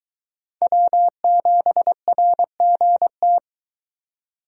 Morse code, heard as a keyed tone, about 23 wpm, 700 Hz. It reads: W7RGT